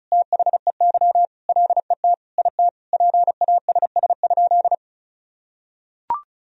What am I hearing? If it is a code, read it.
Morse code, heard as a keyed tone, about 36 wpm, 700 Hz. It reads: THEY LET IT PASS?
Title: Sentences from Top 400 Words and Abbreviations 35wpm